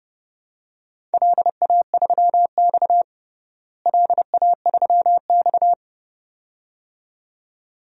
Morse code, heard as a keyed tone, about 30 wpm, 700 Hz. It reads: LA3X LA3X
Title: DXCC-100 Call Signs - 4 Characters - Repeated 2x - 30wpm